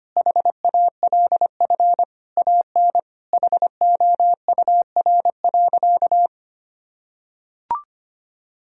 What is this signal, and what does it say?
Morse code, heard as a keyed tone, about 25 wpm, 700 Hz. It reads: HALF AN HOUR.
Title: Sentences from Top 1000 Words - Review Part 1 - 25wpm